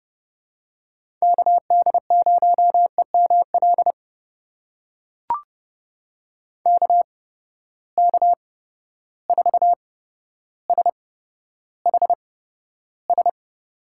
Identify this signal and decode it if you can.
Morse code, heard as a keyed tone, about 30 wpm, spaced as for 13 wpm, 700 Hz. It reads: KD0EML KK4SHS
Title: USA Call Signs - 6 Characters - 8x Character Spacing - 30wpm